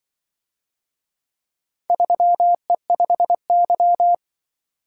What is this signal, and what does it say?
Morse code, heard as a keyed tone, about 24 wpm, 700 Hz. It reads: 3E5Y
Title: Speed-Racing - Four Letter-Number Combinations - 20wpm